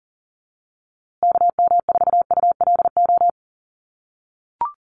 Morse code, heard as a keyed tone, about 40 wpm, 700 Hz. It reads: KM4ULO
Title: USA Call Signs Repeated 3x 40wpm